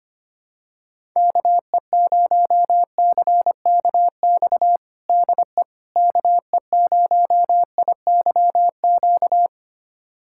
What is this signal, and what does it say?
Morse code, heard as a keyed tone, about 25 wpm, 700 Hz. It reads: KE0CKX DE KE0IYQ
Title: USA Callsign DE Callsign - 6 Characters - 25wpm